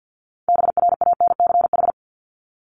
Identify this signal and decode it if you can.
Morse code, heard as a keyed tone, about 50 wpm, 700 Hz. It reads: BRANCH
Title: Top 1751-2000 Words 50wpm